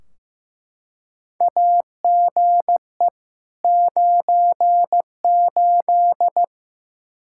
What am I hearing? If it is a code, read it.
Morse code, heard as a keyed tone, about 16 wpm, 700 Hz. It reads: AGE 98
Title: Practice QSO - Exchange 3 - More Info - 15wpm